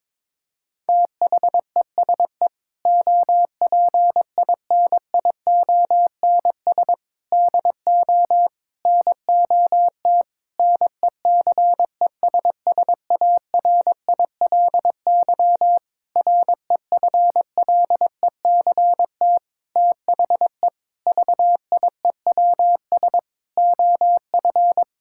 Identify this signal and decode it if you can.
Morse code, heard as a keyed tone, about 22 wpm, 700 Hz. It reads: THESE OPINIONS DO NOT NECESSARILY REFLECT THE VIEWS OF
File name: practice / 22wpm / Sentences with 8-Word Phrases - 22wpm